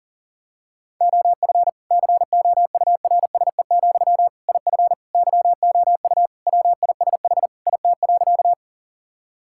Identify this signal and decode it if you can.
Morse code, heard as a keyed tone, about 37 wpm, 700 Hz. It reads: OF COURSE, IF YOU WISH IT.
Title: Sentences from Top 1600 Words 40wpm